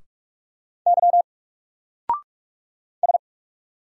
Morse code, heard as a keyed tone, about 45 wpm, 700 Hz. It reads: Y S
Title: Single Letter-Number 45wpm